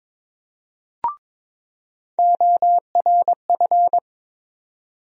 Morse code, heard as a keyed tone, about 22 wpm, 700 Hz. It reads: ORF
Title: USA Call Sign Postfixes - 3 Letters 22wpm